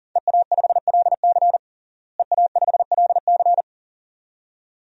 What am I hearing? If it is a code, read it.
Morse code, heard as a keyed tone, about 40 wpm, 700 Hz. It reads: EA5LC EA5LC
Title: DXCC-100 Call Signs - 5 Characters - Repeated 2x - 40wpm